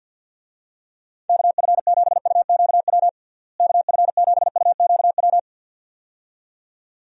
Morse code, heard as a keyed tone, about 50 wpm, 700 Hz. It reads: KF7UQW KF7UQW
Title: USA Call Signs - 6 Characters - Repeated 2x - 50wpm